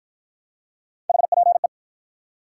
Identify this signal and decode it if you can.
Morse code, heard as a keyed tone, about 49 wpm, 700 Hz. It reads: HPE